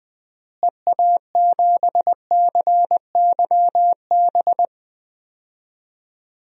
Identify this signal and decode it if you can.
Morse code, heard as a keyed tone, about 20 wpm, 700 Hz. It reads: EA7CYB
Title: DXCC-100 Call Signs - 6 Characters - Repeated 3x - 20wpm